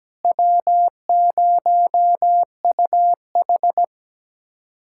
Morse code, heard as a keyed tone, about 17 wpm, 700 Hz. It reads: W0UH
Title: USA Callsign DE Callsign - 4 Characters - 17wpm